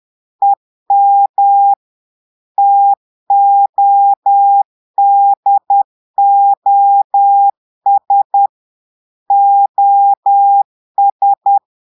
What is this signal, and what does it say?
Morse code, heard as a keyed tone, about 10 wpm, 800 Hz. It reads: EM TODOS OS
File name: portugais texte en clair 2 vitesse 10 mots minutes